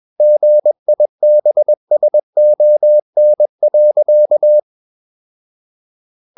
Morse code, heard as a keyed tone, about 21 wpm, 600 Hz. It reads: GIBSON.
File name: Ferie QRS 21wpm